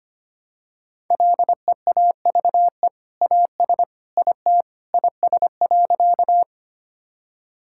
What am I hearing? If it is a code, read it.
Morse code, heard as a keyed tone, about 25 wpm, 690 Hz. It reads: LEAVE AS IT IS.
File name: practice / 25wpm / Sentences with 2-Word Phrases - 25wpm